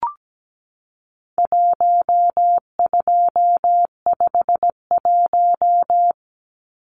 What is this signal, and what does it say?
Morse code, heard as a keyed tone, about 17 wpm, 700 Hz. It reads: 1251
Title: Numbers - Contest Style - 4-Digit 17wpm